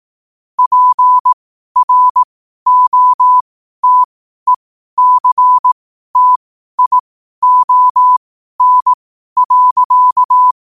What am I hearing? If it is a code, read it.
Morse code, heard as a keyed tone, about 18 wpm, spaced as for 12 wpm, 1000 Hz. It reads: PROTECTION.
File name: qotdfiles / 12wpm